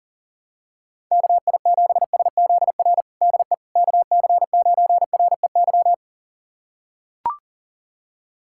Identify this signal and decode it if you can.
Morse code, heard as a keyed tone, about 40 wpm, 700 Hz. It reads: KI7SZR DE KC9REY
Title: USA Callsign DE Callsign - 6 Characters - 40wpm